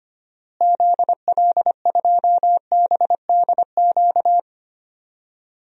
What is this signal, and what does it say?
Morse code, heard as a keyed tone, about 25 wpm, 700 Hz. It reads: ZL2BDQ
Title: DXCC-100 Call Signs - 6 Characters - Repeated 2x - 25wpm